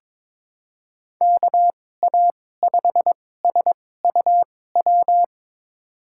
Morse code, heard as a keyed tone, about 22 wpm, spaced as for 16 wpm, 700 Hz. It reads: KA5SUW
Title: USA Call Signs - 2x Character Spacing - 22wpm